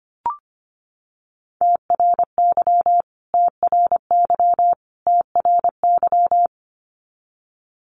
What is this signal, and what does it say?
Morse code, heard as a keyed tone, about 25 wpm, 700 Hz. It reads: TRY TRY TRY
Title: Top 300 Words Repeated 25wpm